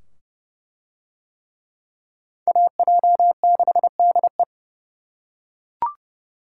Morse code, heard as a keyed tone, about 29 wpm, 700 Hz. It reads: AJ6DE